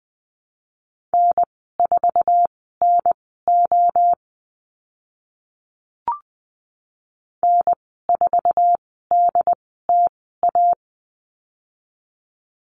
Morse code, heard as a keyed tone, about 20 wpm, spaced as for 14 wpm, 700 Hz. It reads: N4NO N4DTA
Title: 069 Lesson 12g - Letter d - Call Signs 20wpm